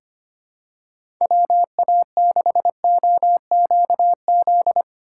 Morse code, heard as a keyed tone, about 25 wpm, 700 Hz. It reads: WA6OQZ
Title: USA Call Signs - 2x Character Spacing - 25wpm